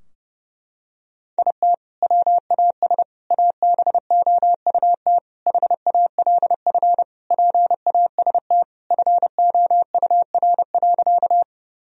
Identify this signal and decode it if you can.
Morse code, heard as a keyed tone, about 32 wpm, 700 Hz. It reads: IT WAS ABOUT HALF PAST FOUR.
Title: Sentences from Top 1000 Words - Review Part 3 - 30wpm